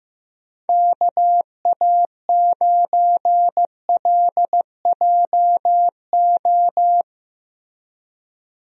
Morse code, heard as a keyed tone, about 15 wpm, 700 Hz. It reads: KA9LJO
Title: USA Call Signs - 6 Characters - Repeated 2x - 15wpm